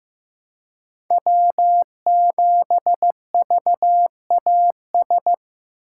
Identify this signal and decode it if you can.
Morse code, heard as a keyed tone, about 15 wpm, 700 Hz. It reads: W7VAS
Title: K1USN SST QSO Practice - 15wpm